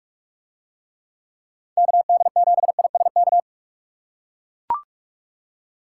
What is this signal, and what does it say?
Morse code, heard as a keyed tone, about 43 wpm, 700 Hz. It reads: KD7ISK